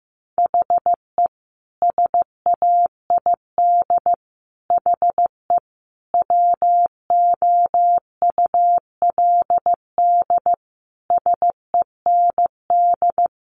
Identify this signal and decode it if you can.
Morse code, heard as a keyed tone, about 15 wpm, 700 Hz. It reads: HE SAID HE WOULD SEND
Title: Sentences from Top 1500 Words 15wpm